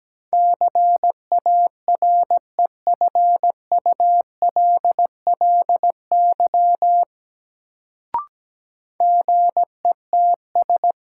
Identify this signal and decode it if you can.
Morse code, heard as a keyed tone, about 17 wpm, 700 Hz. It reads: CAREFULLY GETS
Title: Sentences from Top 1300 Words - New Word Drill Alt - 17wpm